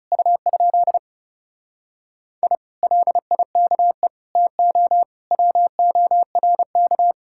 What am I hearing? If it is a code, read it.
Morse code, heard as a keyed tone, about 40 wpm, 700 Hz. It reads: U? I LIKE TO WORK